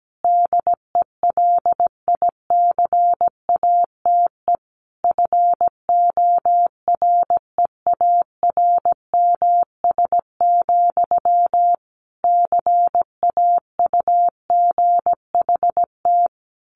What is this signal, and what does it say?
Morse code, heard as a keyed tone, about 17 wpm, 700 Hz. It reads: DELICATE FOREARMS, CAUGHT
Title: A Sound Of Thunder 17wpm